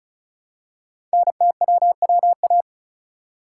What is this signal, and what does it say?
Morse code, heard as a keyed tone, about 36 wpm, 690 Hz. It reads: NTWWA
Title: Code Groups - 5 Letters - 35wpm